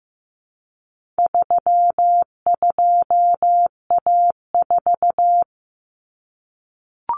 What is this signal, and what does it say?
Morse code, heard as a keyed tone, about 14 wpm, 700 Hz. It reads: 32A4
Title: Common Cut Numbers - 4-Digit 15wpm